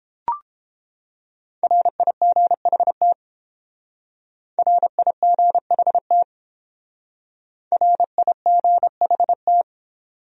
Morse code, heard as a keyed tone, about 28 wpm, 700 Hz. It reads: RIGHT RIGHT RIGHT